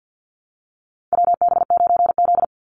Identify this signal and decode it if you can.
Morse code, heard as a keyed tone, about 49 wpm, 700 Hz. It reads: WB9Z
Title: Call Signs Repeated 2x - 50wpm